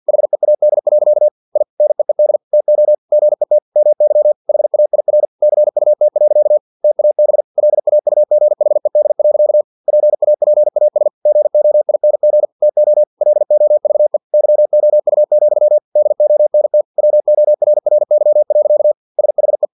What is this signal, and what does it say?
Morse code, heard as a keyed tone, about 50 wpm, 600 Hz. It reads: HEAD. I NEED TO GET MY HAIR CUT. TAB LAUGHED. PAPAS GOING TO LOVE YOU, DONT WORRY. SHE
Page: Never 49wpm